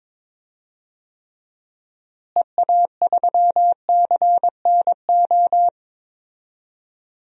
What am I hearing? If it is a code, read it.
Morse code, heard as a keyed tone, about 22 wpm, 700 Hz. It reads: EA3CNO
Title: DXCC-100 Call Signs - 6 Characters - Repeated 2x - 22wpm